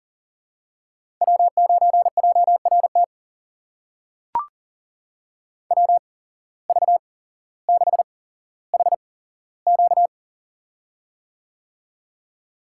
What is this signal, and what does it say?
Morse code, heard as a keyed tone, about 38 wpm, spaced as for 17 wpm, 700 Hz. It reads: W9JRT WV6HQ
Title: USA Call Signs - 5 Characters - 8x Character Spacing - 40wpm